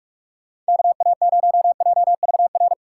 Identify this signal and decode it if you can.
Morse code, heard as a keyed tone, about 60 wpm, spaced as for 52 wpm, 700 Hz. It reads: KA0JVR
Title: Speed Racing - USA 2x3 Call Signs 45wpm